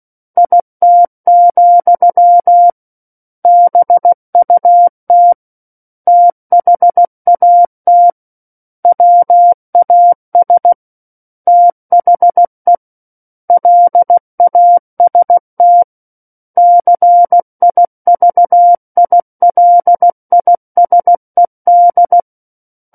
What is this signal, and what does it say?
Morse code, heard as a keyed tone, about 16 wpm, 700 Hz. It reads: IT, BUT THAT WAS THE LAST CIVILISED
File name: Book One / 07-Chapter 7 - 16 WPM